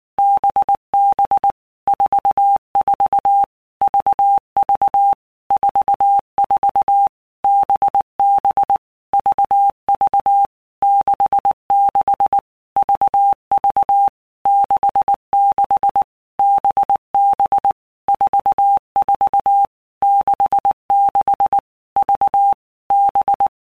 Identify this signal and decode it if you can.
Morse code, heard as a keyed tone, about 19 wpm, 800 Hz. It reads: BB 44 VV 44 BB VV 66 VV 66 BB 44 66 V B
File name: Basic Letter Groups By Blocks / Groups@18 / VB46@18